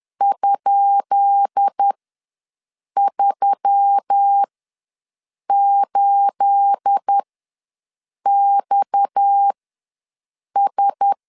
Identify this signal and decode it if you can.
Morse code, heard as a keyed tone, about 10 wpm, spaced as for 6 wpm, 800 Hz. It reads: ?38XS
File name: LEZIONI / 10 SIGNS / 10B INT ws 10wpm 4134